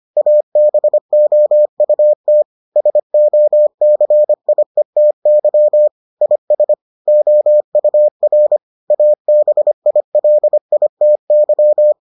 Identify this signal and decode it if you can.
Morse code, heard as a keyed tone, about 25 wpm, 600 Hz. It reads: ABOUT SOCIETY IS OUR ABILITY